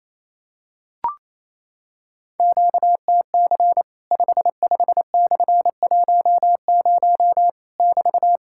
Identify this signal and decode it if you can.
Morse code, heard as a keyed tone, about 28 wpm, 700 Hz. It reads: QTC 55/10 <BT>
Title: Worked All Europe DX Contest - QTC Exchange - 28wpm